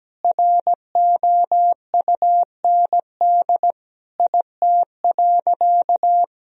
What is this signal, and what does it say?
Morse code, heard as a keyed tone, about 17 wpm, 700 Hz. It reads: ROUND IT.
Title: Sentences from Top 1000 Words - Review Part 1 - 17wpm